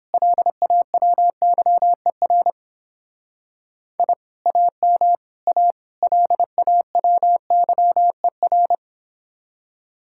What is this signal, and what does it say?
Morse code, heard as a keyed tone, about 31 wpm, 700 Hz. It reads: LAWYER I AM A LAWYER